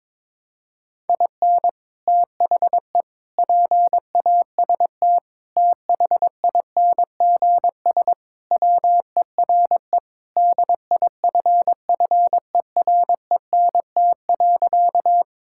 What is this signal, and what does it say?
Morse code, heard as a keyed tone, about 22 wpm, 700 Hz. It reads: IN THE PAST THINGS WERE DIFFERENT.
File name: practice / 22wpm / Sentences with 3-Word Phrases - 22wpm